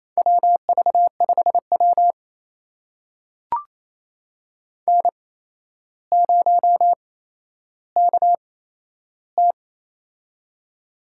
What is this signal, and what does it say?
Morse code, heard as a keyed tone, about 27 wpm, spaced as for 12 wpm, 700 Hz. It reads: WV5W N0KT
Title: USA Call Signs - 4 Characters - 8x Character Spacing - 28wpm